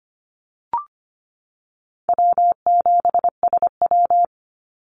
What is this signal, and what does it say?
Morse code, heard as a keyed tone, about 25 wpm, 700 Hz. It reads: W7SW
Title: Easy Call Signs - 25wpm